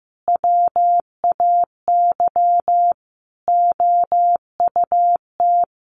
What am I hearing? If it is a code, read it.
Morse code, heard as a keyed tone, about 15 wpm, 700 Hz. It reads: WAY OUT
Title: Sentences from Top 1400 Words 15wpm